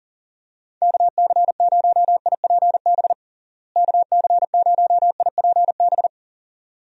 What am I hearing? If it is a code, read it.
Morse code, heard as a keyed tone, about 40 wpm, 700 Hz. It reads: KC0IPB KC0IPB
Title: USA Call Signs - 6 Characters - Repeated 2x - 40wpm